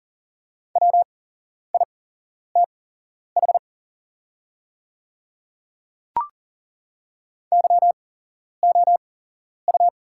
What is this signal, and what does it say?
Morse code, heard as a keyed tone, about 40 wpm, spaced as for 11 wpm, 700 Hz. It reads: WITH YOU
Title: 8x Character Spacing - Top 100 Words - 40wpm